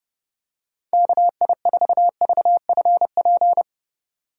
Morse code, heard as a keyed tone, about 31 wpm, 700 Hz. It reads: KI4VFP
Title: USA Call Signs - 6 Characters - 3x Character Spacing - 30wpm